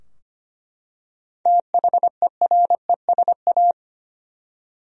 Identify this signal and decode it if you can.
Morse code, heard as a keyed tone, about 25 wpm, 700 Hz. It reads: THERESA
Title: USA Common Names - 25wpm